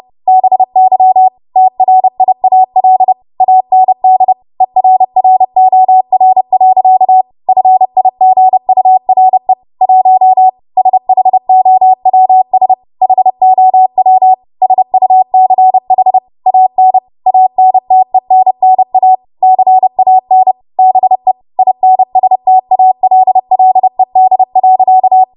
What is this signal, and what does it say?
Morse code, heard as a keyed tone, about 30 wpm, 750 Hz. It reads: BY TRIAL AND ERROR. FIGURE 1 SHOWS HOW SUCH AN ANTENNA CAN BE INSTALLED.
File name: Morse / 30WPM